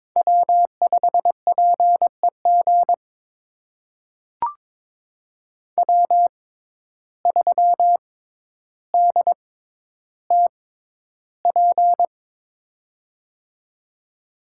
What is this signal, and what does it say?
Morse code, heard as a keyed tone, about 22 wpm, spaced as for 11 wpm, 700 Hz. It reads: W5PEG W3DTP
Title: USA Call Signs - 6x Character Spacing - 22wpm